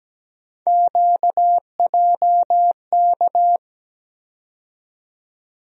Morse code, heard as a keyed tone, about 17 wpm, 700 Hz. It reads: QJK
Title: USA Call Sign Postfixes - 3 Letters 17wpm